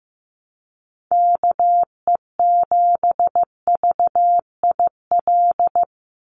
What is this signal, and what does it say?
Morse code, heard as a keyed tone, about 15 wpm, 700 Hz. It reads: KE7VIL
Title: USA Call Signs 15wpm